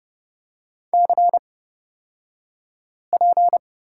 Morse code, heard as a keyed tone, about 30 wpm, spaced as for 5 wpm, 690 Hz. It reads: CP
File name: Lesson 36d - BT - Two Characters 30wpm